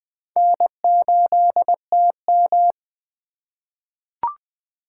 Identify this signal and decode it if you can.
Morse code, heard as a keyed tone, about 20 wpm, 700 Hz. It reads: N8TM
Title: USA Call Signs - 4 Characters - 4x Character Spacing - 20wpm